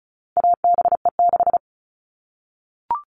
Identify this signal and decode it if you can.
Morse code, heard as a keyed tone, about 34 wpm, 700 Hz. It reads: ABE6
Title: All Cut Numbers - 4-Digit 35wpm